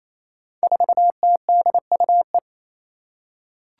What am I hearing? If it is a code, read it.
Morse code, heard as a keyed tone, about 27 wpm, 700 Hz. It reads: 4TDUE
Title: Five Challenging Letter-Number Combinations 28wpm